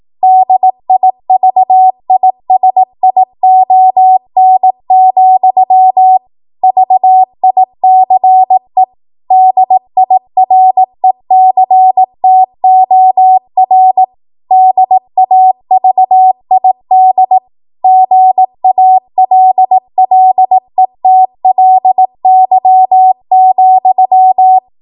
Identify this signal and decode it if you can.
Morse code, heard as a keyed tone, about 18 wpm, 750 Hz. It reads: DIVISION, VICE DIRECTOR DAVID GALLETLY,